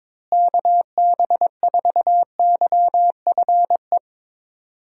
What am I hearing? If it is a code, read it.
Morse code, heard as a keyed tone, about 22 wpm, 700 Hz. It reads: KB4YFE
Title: USA Call Signs - 6 Characters - Repeated 2x - 22wpm